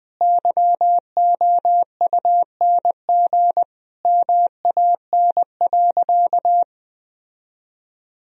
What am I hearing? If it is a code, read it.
Morse code, heard as a keyed tone, about 20 wpm, 700 Hz. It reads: YOUNG MAN.
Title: Sentences from Top 1000 Words - Review Part 1 - 20wpm